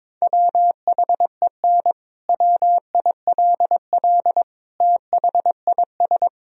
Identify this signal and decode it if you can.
Morse code, heard as a keyed tone, about 21 wpm, 700 Hz. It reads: WHEN WILL THIS
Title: Sentences from Top 1000 Words - Review Part 4 - 22wpm